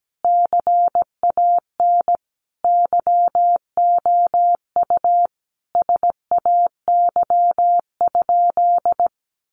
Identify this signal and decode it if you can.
Morse code, heard as a keyed tone, about 17 wpm, 700 Hz. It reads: CAN YOU SAY?
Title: Sentences from Top 800 Words 17wpm